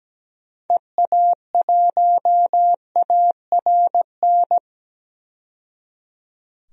Morse code, heard as a keyed tone, about 17 wpm, 700 Hz. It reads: EA1ARN
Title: DXCC-100 Call Signs - 6 Characters - 17wpm